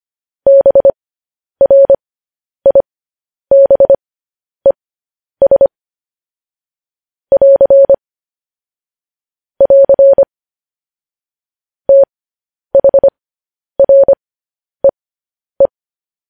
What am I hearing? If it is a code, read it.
Morse code, heard as a keyed tone, about 25 wpm, spaced as for 10 wpm, 550 Hz. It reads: BRIBES <AR> <AR> THREE
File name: podcast / News headlines-1x-25x10